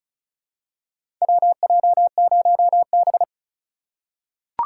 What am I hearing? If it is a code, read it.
Morse code, heard as a keyed tone, about 36 wpm, 700 Hz. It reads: WJ0B
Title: USA Call Signs - 4 Characters - 3x Character Spacing - 35wpm